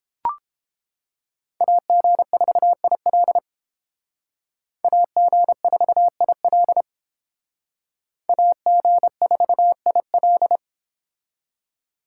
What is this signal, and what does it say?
Morse code, heard as a keyed tone, about 30 wpm, 700 Hz. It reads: AG4IL AG4IL AG4IL